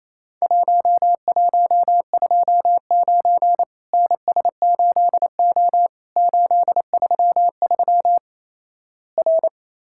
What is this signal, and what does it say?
Morse code, heard as a keyed tone, about 28 wpm, 700 Hz. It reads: 1129 NS8O 833 R
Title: Worked All Europe DX Contest - QTC Exchange - 28wpm